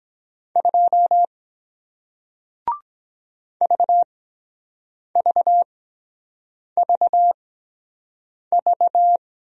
Morse code, heard as a keyed tone, about 23 wpm, spaced as for 7 wpm, 700 Hz. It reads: 2 VVVV